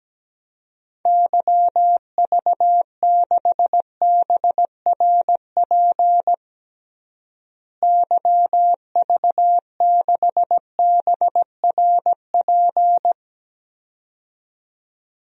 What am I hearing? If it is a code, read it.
Morse code, heard as a keyed tone, about 17 wpm, 700 Hz. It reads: YV6BRP YV6BRP
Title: DXCC-100 Call Signs - 6 Characters - Repeated 2x - 17wpm